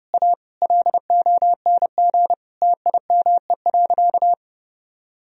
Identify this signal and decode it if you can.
Morse code, heard as a keyed tone, about 28 wpm, 700 Hz. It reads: A LONG TIME.
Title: Sentences from Top 1000 Words - Review Part 1 - 30wpm